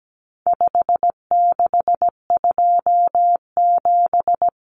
Morse code, heard as a keyed tone, about 17 wpm, 700 Hz. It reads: 5627
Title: Common Cut Numbers - 4-Digit 17wpm